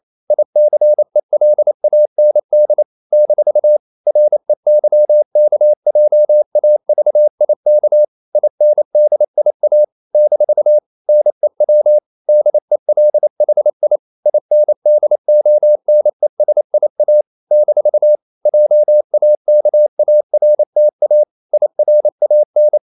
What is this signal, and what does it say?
Morse code, heard as a keyed tone, about 28 wpm, 600 Hz. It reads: ICELAND - REYKJAVIK INDIA - NEW DELHI INDONESIA - JAKARTA IRAN
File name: CW / Practice / NationalCapitals28